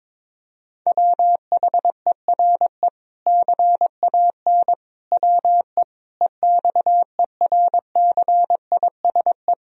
Morse code, heard as a keyed tone, about 22 wpm, 700 Hz. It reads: WHERE CAN WE EXERCISE
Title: Sentences from Top 1000 Words - Review Part 4 - 22wpm